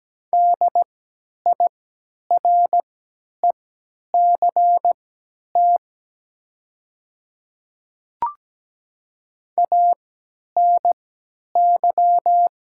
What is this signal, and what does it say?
Morse code, heard as a keyed tone, about 17 wpm, spaced as for 10 wpm, 700 Hz. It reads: DIRECT ANY
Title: 3x Character Spacing - Top 500 Words - 17wpm